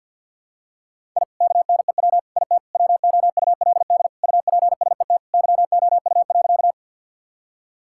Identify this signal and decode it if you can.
Morse code, heard as a keyed tone, about 50 wpm, 700 Hz. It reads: I KNEW IT WOULD UPSET YOU.
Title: Sentences from Top 1000 Words - Review Part 5 - 50wpm